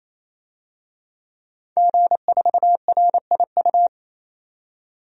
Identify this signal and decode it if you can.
Morse code, heard as a keyed tone, about 28 wpm, 700 Hz. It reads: G4RIU
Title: DXCC-100 Call Signs - 5 Characters - 28wpm